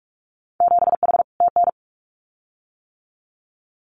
Morse code, heard as a keyed tone, about 46 wpm, 690 Hz. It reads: 7H TN
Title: Winter Field Day Exchanges Encore 45wpm